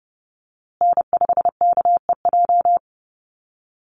Morse code, heard as a keyed tone, about 30 wpm, 700 Hz. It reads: N5KEJ
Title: USA Call Signs Repeated 2x 30wpm